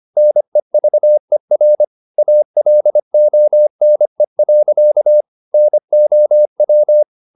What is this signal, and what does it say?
Morse code, heard as a keyed tone, about 25 wpm, 600 Hz. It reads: NEVER ALONE. NOW
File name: the old man and the sea part 1 25wpm